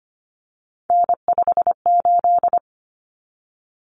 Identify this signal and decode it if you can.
Morse code, heard as a keyed tone, about 25 wpm, 700 Hz. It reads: N58
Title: Common Cut Numbers - 3-Digit 25wpm